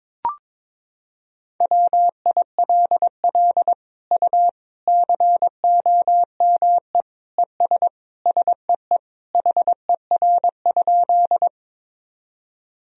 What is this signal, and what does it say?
Morse code, heard as a keyed tone, about 22 wpm, 700 Hz. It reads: WILL U COME ES SEE HER?
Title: Sentences from Top 100 Words and Abbreviations 22wpm